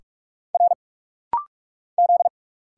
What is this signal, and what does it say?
Morse code, heard as a keyed tone, about 46 wpm, 700 Hz. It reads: R Z
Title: Single Letters - Rapid-Fire 45wpm